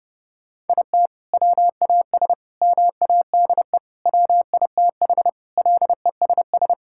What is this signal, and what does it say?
Morse code, heard as a keyed tone, about 30 wpm, 700 Hz. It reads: IT WAS MADE WITH LESS
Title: Sentences from Top 1000 Words - Review Part 5 - 30wpm